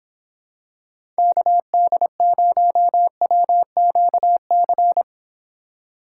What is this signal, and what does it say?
Morse code, heard as a keyed tone, about 26 wpm, 700 Hz. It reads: KD0WQC